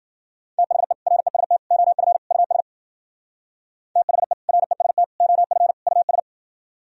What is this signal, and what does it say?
Morse code, heard as a keyed tone, about 49 wpm, 700 Hz. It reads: THE REST OF US THE REST OF US
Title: Speed-Racing - Sets of 4 Words 40wpm